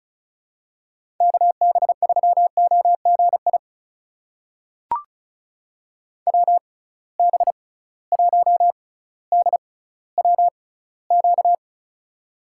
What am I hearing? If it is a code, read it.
Morse code, heard as a keyed tone, about 36 wpm, spaced as for 18 wpm, 700 Hz. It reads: KB3OGI WB1DWQ
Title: USA Call Signs - 6 Characters - 6x Character Spacing - 35wpm